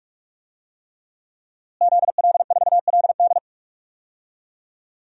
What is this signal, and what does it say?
Morse code, heard as a keyed tone, about 46 wpm, 700 Hz. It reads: ZL4LD
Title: DXCC-100 Call Signs - 5 Characters - Repeated 3x - 45wpm